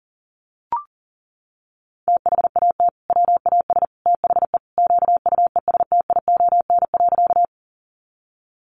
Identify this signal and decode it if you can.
Morse code, heard as a keyed tone, about 40 wpm, 700 Hz. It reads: THAT WAS THE QUESTION.
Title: Sentences from Top 400 Words 40wpm